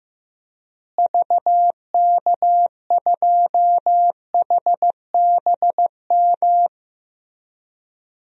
Morse code, heard as a keyed tone, about 15 wpm, 700 Hz. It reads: VK2HBM
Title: DXCC-100 Call Signs - 6 Characters - Repeated 3x - 15wpm